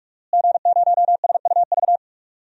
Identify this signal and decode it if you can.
Morse code, heard as a keyed tone, about 46 wpm, 700 Hz. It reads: G0SUV
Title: DXCC-100 Call Signs - 5 Characters - Repeated 2x - 45wpm